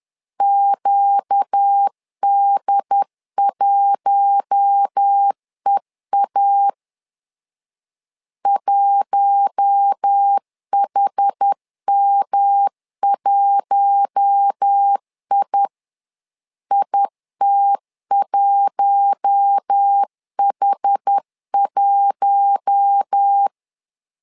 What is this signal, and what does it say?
Morse code, heard as a keyed tone, about 10 wpm, 800 Hz. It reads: QD1EA 1HM1I IT1H1
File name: LEZIONI / 4 Q1 10wpm 2448